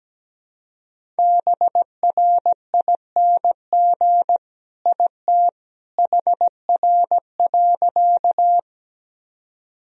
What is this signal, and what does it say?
Morse code, heard as a keyed tone, about 17 wpm, 700 Hz. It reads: BRING IT HR.
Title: Sentences from Top 500 Words and Abbreviations 17wpm